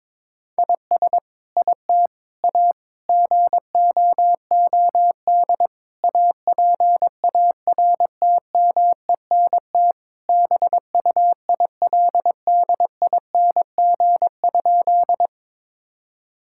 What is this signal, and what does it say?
Morse code, heard as a keyed tone, about 22 wpm, 700 Hz. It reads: IS IT A GOOD APARTMENT BUILDING?
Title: Sentences from Top 1000 Words - Review Part 6 - 22wpm